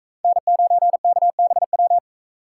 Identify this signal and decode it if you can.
Morse code, heard as a keyed tone, about 40 wpm, 700 Hz. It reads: N9KBW